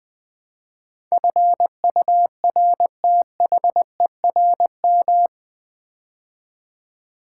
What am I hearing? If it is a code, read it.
Morse code, heard as a keyed tone, about 20 wpm, 700 Hz. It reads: FURTHERM
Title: Build A Word - 4 Letters At A Time - 11-Letter Words - 20wpm